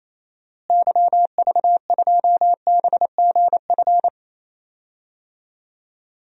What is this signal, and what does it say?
Morse code, heard as a keyed tone, about 28 wpm, 700 Hz. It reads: YV2BGF
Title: DXCC-100 Call Signs - 6 Characters - Repeated 2x - 28wpm